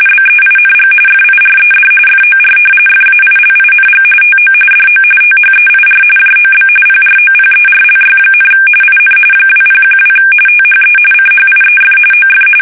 RTTY, about 45.45 baud. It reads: BEING FIRED FOR FAILING TO DO THEIR JOB PROPERLY, IN 2012 THE CHIEF OF
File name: fsk850 / usnavy5